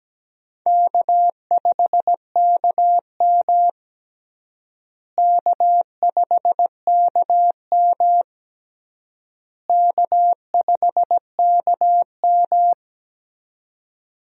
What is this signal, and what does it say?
Morse code, heard as a keyed tone, about 17 wpm, 700 Hz. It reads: K5KM K5KM K5KM
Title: USA Call Signs - 4 Characters - Repeated 3x - 17wpm